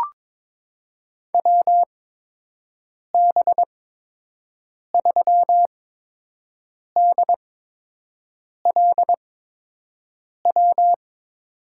Morse code, heard as a keyed tone, about 21 wpm, spaced as for 6 wpm, 700 Hz. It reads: WB3DLW
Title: USA Call Signs - 6 Characters - 8x Character Spacing - 22wpm